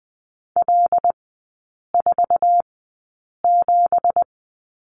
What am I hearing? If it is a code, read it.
Morse code, heard as a keyed tone, about 20 wpm, spaced as for 8 wpm, 700 Hz. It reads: L47
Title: 166 Lesson 27e - Number 7 - Three Characters 20wpm